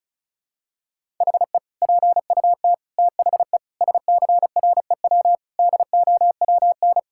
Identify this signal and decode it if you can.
Morse code, heard as a keyed tone, about 36 wpm, 700 Hz. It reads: HE PUT THE SCREW DOWN
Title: Sentences from Top 1600 Words 35wpm